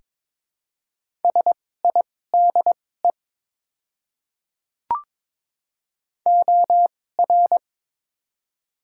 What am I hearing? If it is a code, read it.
Morse code, heard as a keyed tone, about 22 wpm, spaced as for 16 wpm, 690 Hz. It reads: SIDE OR
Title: 2x Character Spacing - Top 100 Words - 22wpm